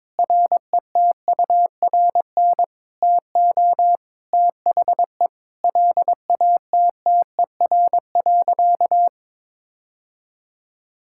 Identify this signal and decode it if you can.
Morse code, heard as a keyed tone, about 22 wpm, 700 Hz. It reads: RETURN TO THE LATTER.
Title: Sentences from Top 1600 Words 22wpm